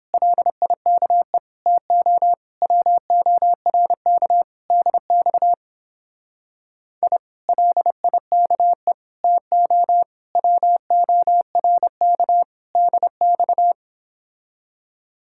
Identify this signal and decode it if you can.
Morse code, heard as a keyed tone, about 29 wpm, 700 Hz. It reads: LIKE TO WORK DX I LIKE TO WORK DX